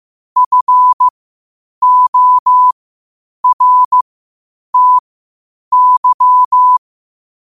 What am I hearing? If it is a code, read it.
Morse code, heard as a keyed tone, about 15 wpm, spaced as for 8 wpm, 1000 Hz. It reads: FORTY
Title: 8 WPM morse code quotes for Fri, 01 Mar 2024